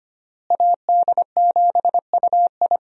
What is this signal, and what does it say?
Morse code, heard as a keyed tone, about 25 wpm, 700 Hz. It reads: AD7UI
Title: USA Call Signs - 5 Characters - Repeated 3x - 25wpm